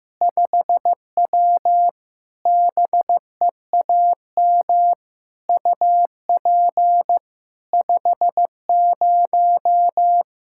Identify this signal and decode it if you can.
Morse code, heard as a keyed tone, about 16 wpm, 700 Hz. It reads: 5W BEAM UP 50
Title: Practice QSO - Exchange 2 - Info - 15wpm